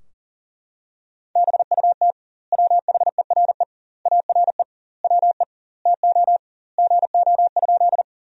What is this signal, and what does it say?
Morse code, heard as a keyed tone, about 37 wpm, 700 Hz. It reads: BUT WHERE ARE WE TO GO?
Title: 2x Word-Spacing - Sentences from Top 200 Words - 40wpm